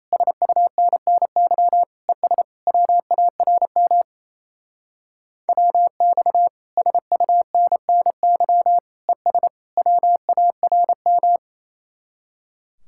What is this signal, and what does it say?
Morse code, heard as a keyed tone, about 34 wpm, 700 Hz. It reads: SUNNY ES WARM WX SUNNY ES WARM
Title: Speed-Racing Common QSO Phrases Encore - 28wpm